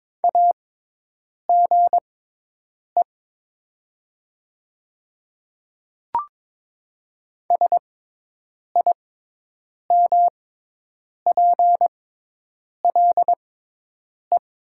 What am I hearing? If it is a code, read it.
Morse code, heard as a keyed tone, about 22 wpm, spaced as for 8 wpm, 700 Hz. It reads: AGE SIMPLE
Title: 6x Character Spacing - Top 500 Words - 22wpm